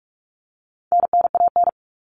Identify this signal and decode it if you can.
Morse code, heard as a keyed tone, about 44 wpm, 700 Hz. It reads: NNAN
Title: All Cut Numbers - 4-Digit 45wpm